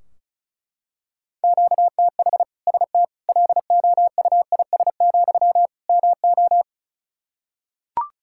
Morse code, heard as a keyed tone, about 36 wpm, 700 Hz. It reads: QTH ST LOUIS, MO
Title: Practice QSO - Exchange 1 - Beginning - 35wpm